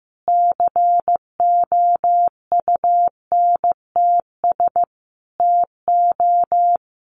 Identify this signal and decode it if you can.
Morse code, heard as a keyed tone, about 15 wpm, 700 Hz. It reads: COUNTS TO
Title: Easy Phrases 15wpm